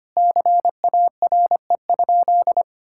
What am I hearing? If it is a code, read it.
Morse code, heard as a keyed tone, about 25 wpm, 700 Hz. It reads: CARE?
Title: Sentences from Top 1000 Words - Review Part 5 - 25wpm